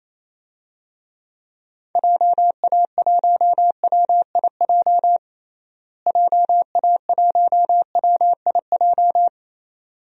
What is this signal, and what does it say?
Morse code, heard as a keyed tone, about 28 wpm, 700 Hz. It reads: JA1WIJ JA1WIJ
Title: DXCC-100 Call Signs - 6 Characters - Repeated 2x - 28wpm